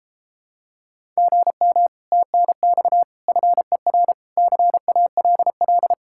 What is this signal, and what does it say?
Morse code, heard as a keyed tone, about 32 wpm, 700 Hz. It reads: GM TNX FER CALL